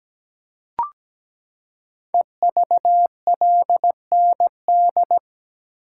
Morse code, heard as a keyed tone, about 17 wpm, 700 Hz. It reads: EVLND
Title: Code Groups - 5 Letter:Numbers - 17wpm